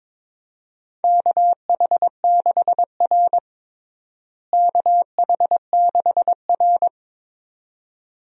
Morse code, heard as a keyed tone, about 22 wpm, 700 Hz. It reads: KH6R KH6R
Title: DXCC-100 Call Signs - 4 Characters - Repeated 2x - 22wpm